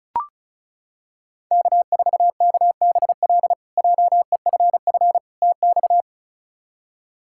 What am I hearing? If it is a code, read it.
Morse code, heard as a keyed tone, about 36 wpm, 700 Hz. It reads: K4KBL JEFF TX
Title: K1USN SST QSO Practice - 35wpm